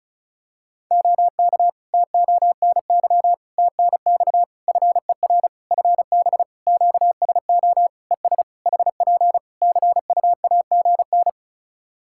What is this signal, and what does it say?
Morse code, heard as a keyed tone, about 34 wpm, 700 Hz. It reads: OK TONY TNX FER FB QSO ES HP CUAGN
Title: Practice QSO - Exchange 4 - Ending - 35wpm